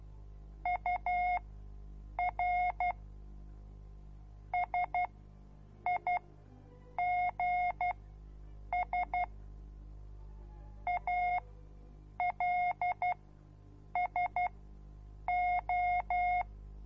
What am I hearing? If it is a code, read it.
Morse code, heard as a keyed tone, about 12 wpm, spaced as for 7 wpm, 720 Hz. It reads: UR SIGS ALSO